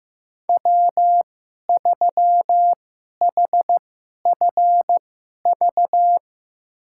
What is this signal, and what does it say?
Morse code, heard as a keyed tone, about 15 wpm, spaced as for 11 wpm, 700 Hz. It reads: W3HFV
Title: USA Call Signs - 5 Characters - 2x Character Spacing - 15wpm